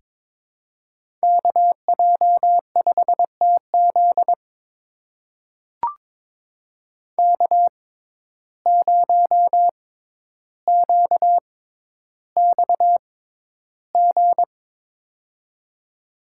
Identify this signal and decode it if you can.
Morse code, heard as a keyed tone, about 22 wpm, spaced as for 11 wpm, 700 Hz. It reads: KJ5TZ K0QXG
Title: USA Call Signs - 5 Characters - 6x Character Spacing - 22wpm